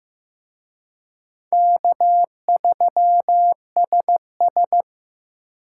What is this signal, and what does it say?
Morse code, heard as a keyed tone, about 18 wpm, 700 Hz. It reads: K3SS
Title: Speed Racing - USA 1x2 Call Signs 15wpm